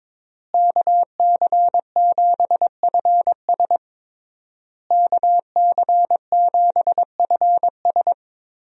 Morse code, heard as a keyed tone, about 22 wpm, 700 Hz. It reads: KC7FS KC7FS
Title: USA Call Signs - 5 Characters - Repeated 3x - 22wpm